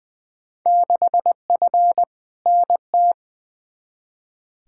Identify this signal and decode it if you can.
Morse code, heard as a keyed tone, about 20 wpm, 700 Hz. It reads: 6F NT
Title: Field Day Exchanges Encore - 20wpm